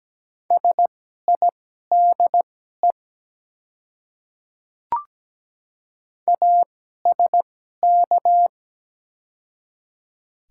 Morse code, heard as a keyed tone, about 17 wpm, spaced as for 12 wpm, 700 Hz. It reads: SIDE ASK
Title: 2x Character Spacing - Top 300 Words - 17wpm